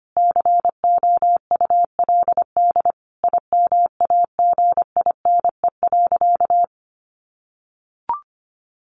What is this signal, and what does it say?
Morse code, heard as a keyed tone, about 25 wpm, 700 Hz. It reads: COULD IMAGINE.
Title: Sentences from Top 1400 Words 25wpm